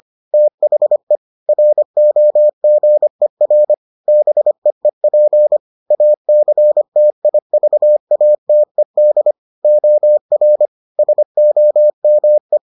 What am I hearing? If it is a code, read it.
Morse code, heard as a keyed tone, about 25 wpm, 600 Hz. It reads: THE ROGER BEEP ACTIVATED OR SOME